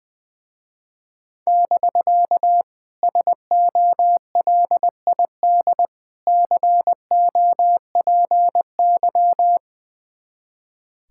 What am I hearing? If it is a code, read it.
Morse code, heard as a keyed tone, about 21 wpm, 700 Hz. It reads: <BK> SOLID COPY
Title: Speed-Racing Common QSO Phrases Encore - 20wpm